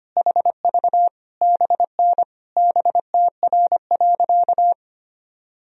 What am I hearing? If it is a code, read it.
Morse code, heard as a keyed tone, about 25 wpm, 700 Hz. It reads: HV BN BTR.
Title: Sentences from Top 500 Words and Abbreviations 25wpm